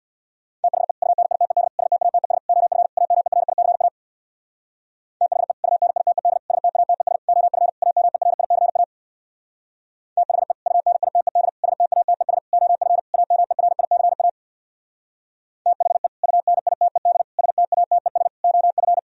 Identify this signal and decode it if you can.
Morse code, heard as a keyed tone, about 64 wpm, 700 Hz. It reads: THE UNITED STATES OF AMERICA THE UNITED STATES OF AMERICA THE UNITED STATES OF AMERICA THE UNITED STATES OF